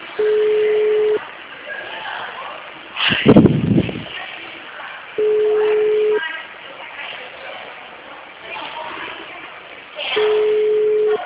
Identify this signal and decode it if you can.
Ringback tone (Europe, 425 Hz), with speech.